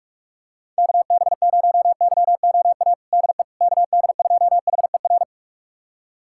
Morse code, heard as a keyed tone, about 46 wpm, 700 Hz. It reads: KB0YOA DE KD2HER
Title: USA Callsign DE Callsign - 6 Characters - 45wpm